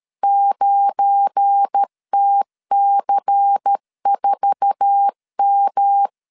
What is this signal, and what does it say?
Morse code, heard as a keyed tone, about 13 wpm, 800 Hz. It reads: 9TC4M
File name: LEZIONI / 8 YJ45 / 8 YJ4 12wpm 2151